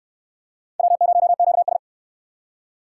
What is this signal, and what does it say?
Morse code, heard as a keyed tone, about 64 wpm, 700 Hz. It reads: W0QS